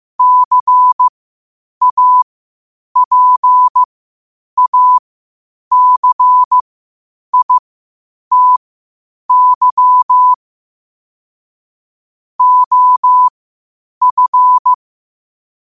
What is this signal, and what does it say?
Morse code, heard as a keyed tone, about 15 wpm, spaced as for 8 wpm, 1000 Hz. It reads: CAPACITY OF
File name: qotdfiles / 8wpm